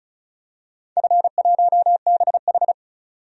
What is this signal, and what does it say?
Morse code, heard as a keyed tone, about 36 wpm, 700 Hz. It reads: F1BH
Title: DXCC-100 Call Signs - 4 Characters - 35wpm